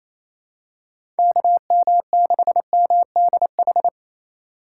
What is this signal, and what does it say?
Morse code, heard as a keyed tone, about 28 wpm, 700 Hz. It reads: KM6MDH
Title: USA Call Signs - 6 Characters - 2x Character Spacing - 28wpm